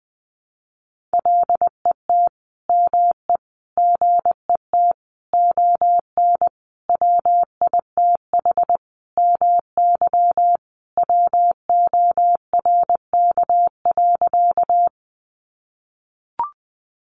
Morse code, heard as a keyed tone, about 19 wpm, 700 Hz. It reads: LET ME GET ON WITH MY WORK.
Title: Sentences from Top 300 Words 20wpm